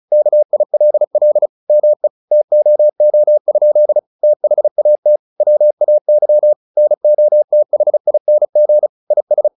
Morse code, heard as a keyed tone, about 35 wpm, 600 Hz. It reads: KILL ME TOO? THAT WAY NOTHING IS
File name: the old man and the sea 2 35wpm